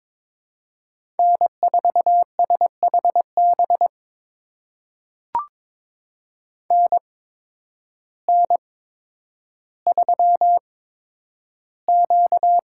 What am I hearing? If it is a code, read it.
Morse code, heard as a keyed tone, about 22 wpm, spaced as for 10 wpm, 700 Hz. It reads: N4SHB NN3Q
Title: USA Call Signs - 8x Character Spacing - 22wpm